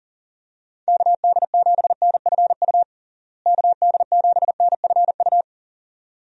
Morse code, heard as a keyed tone, about 40 wpm, 700 Hz. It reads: KD7NFU KD7NFU
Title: USA Call Signs - 6 Characters - Repeated 2x - 40wpm